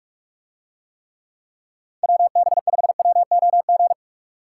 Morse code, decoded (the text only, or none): WB5WOG